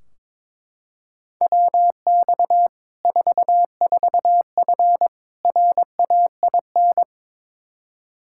Morse code, heard as a keyed tone, about 22 wpm, 700 Hz. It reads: WX 44F RAIN
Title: Practice QSO - Exchange 2 - Info - 22wpm